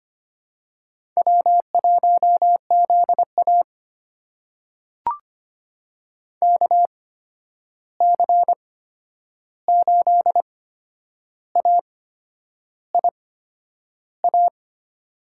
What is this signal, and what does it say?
Morse code, heard as a keyed tone, about 25 wpm, spaced as for 9 wpm, 700 Hz. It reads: W1ZA KC8AIA
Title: USA Call Signs - 8x Character Spacing - 25wpm